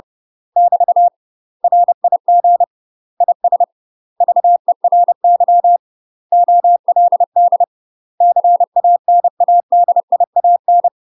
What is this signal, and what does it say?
Morse code, heard as a keyed tone, about 30 wpm, 700 Hz. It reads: <BT> RIG IS VERY OLD CANADIAN
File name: UK MASTER QSOs TRAINING / 30WPM / UK MASTER QSOs 0022